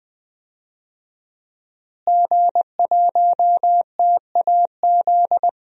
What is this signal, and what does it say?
Morse code, heard as a keyed tone, about 20 wpm, 700 Hz. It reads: G1TAZ
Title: DXCC-100 Call Signs - 5 Characters - Repeated 2x - 20wpm